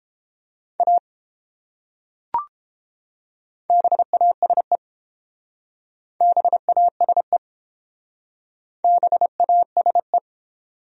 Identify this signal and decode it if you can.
Morse code, heard as a keyed tone, about 30 wpm, 700 Hz. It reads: A BASE BASE BASE